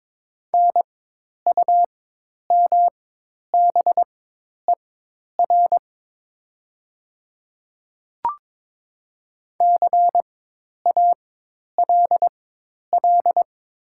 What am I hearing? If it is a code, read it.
Morse code, heard as a keyed tone, about 22 wpm, spaced as for 10 wpm, 700 Hz. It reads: NUMBER CALL
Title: 4x Character Spacing - Top 100 Words - 22wpm